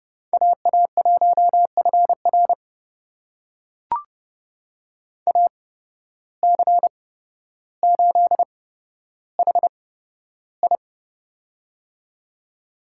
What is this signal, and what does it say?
Morse code, heard as a keyed tone, about 32 wpm, spaced as for 13 wpm, 700 Hz. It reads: AA1FR AC8HI
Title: USA Call Signs - 5 Characters - 8x Character Spacing - 30wpm